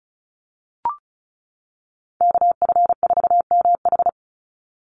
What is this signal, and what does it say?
Morse code, heard as a keyed tone, about 36 wpm, 700 Hz. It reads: KF4MH
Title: NA SOTA Chaser Callsigns 35wpm